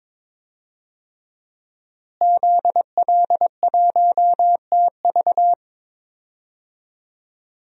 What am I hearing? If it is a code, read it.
Morse code, heard as a keyed tone, about 22 wpm, 700 Hz. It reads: ZL1TV
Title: DXCC-100 Call Signs - 5 Characters - Repeated 3x - 22wpm